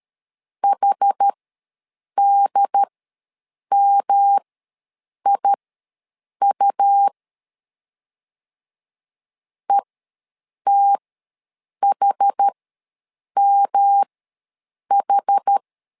Morse code, heard as a keyed tone, about 13 wpm, spaced as for 7 wpm, 800 Hz. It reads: HDMIU ETHMH
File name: LEZIONI / 3 MWRO / 3 M ws 12wpm 1745